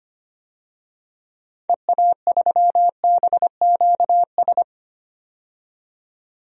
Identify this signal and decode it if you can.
Morse code, heard as a keyed tone, about 25 wpm, 700 Hz. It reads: EA3BQS
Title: DXCC-100 Call Signs - 6 Characters - 25wpm